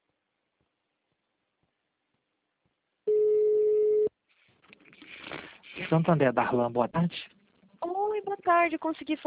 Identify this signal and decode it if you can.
Ringback tone (Europe, 425 Hz), with speech.